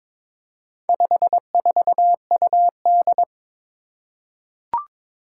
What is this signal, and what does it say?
Morse code, heard as a keyed tone, about 22 wpm, 700 Hz. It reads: 54UD
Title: Four Challenging Letter-Number Combinations 22wpm